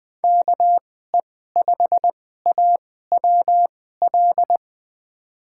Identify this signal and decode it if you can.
Morse code, heard as a keyed tone, about 20 wpm, spaced as for 14 wpm, 700 Hz. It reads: KE5AWL
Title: USA Call Signs - 6 Characters - 2x Character Spacing - 20wpm